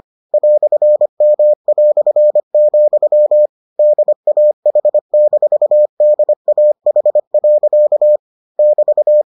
Morse code, heard as a keyed tone, about 25 wpm, 600 Hz. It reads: "M", DAH-DAH. <BT>